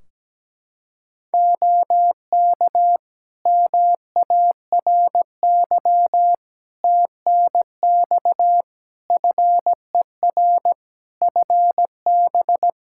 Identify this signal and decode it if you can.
Morse code, heard as a keyed tone, about 17 wpm, 700 Hz. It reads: OK MARY TNX FER FB
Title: Practice QSO - Exchange 4 - Ending - 17wpm